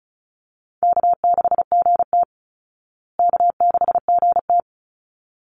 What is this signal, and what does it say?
Morse code, heard as a keyed tone, about 34 wpm, 700 Hz. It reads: K6GT K6GT
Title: Call Signs Repeated 3x - 35wpm